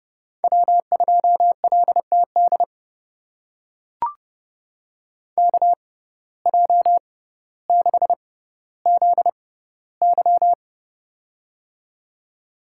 Morse code, heard as a keyed tone, about 32 wpm, spaced as for 16 wpm, 700 Hz. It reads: W2LTD KJ6ZY
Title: USA Call Signs - 5 Characters - 6x Character Spacing - 30wpm